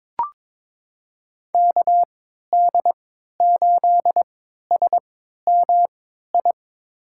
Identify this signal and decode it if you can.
Morse code, heard as a keyed tone, about 21 wpm, spaced as for 12 wpm, 700 Hz. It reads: KD8SMI
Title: USA Call Signs - 3x Character Spacing - 22wpm